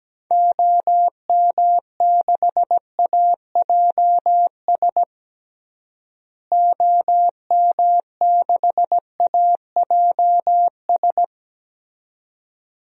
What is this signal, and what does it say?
Morse code, heard as a keyed tone, about 17 wpm, 700 Hz. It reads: OM6AJS OM6AJS
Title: DXCC-100 Call Signs - 6 Characters - Repeated 2x - 17wpm